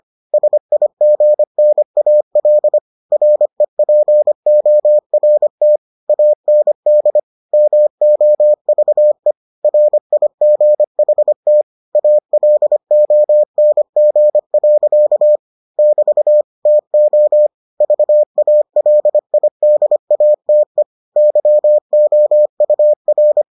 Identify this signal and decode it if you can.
Morse code, heard as a keyed tone, about 25 wpm, 600 Hz. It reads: SIGNAL REPORT AND MOVE RIGHT ALONG. <BT> TO VALIDATE YOUR